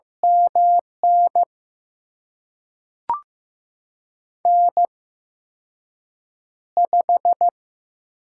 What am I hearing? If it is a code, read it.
Morse code, heard as a keyed tone, about 15 wpm, spaced as for 6 wpm, 700 Hz. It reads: MN N5
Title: USA Call Signs - 8x Character Spacing - 15wpm